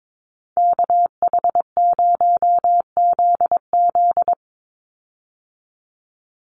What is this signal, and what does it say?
Morse code, heard as a keyed tone, about 22 wpm, 700 Hz. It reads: KH0ZZ
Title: Call Signs - 22wpm